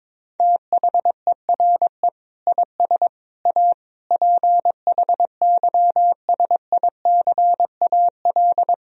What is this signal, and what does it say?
Morse code, heard as a keyed tone, about 21 wpm, 700 Hz. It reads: THERE IS A PHYSICAL
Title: Sentences from Top 1000 Words - Review Part 4 - 22wpm